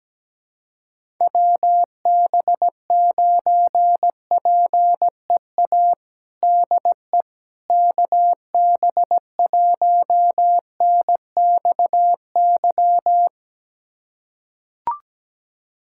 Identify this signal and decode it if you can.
Morse code, heard as a keyed tone, about 17 wpm, 700 Hz. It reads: WB9PEA DE KB1NXY
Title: USA Callsign DE Callsign - 6 Characters - 17wpm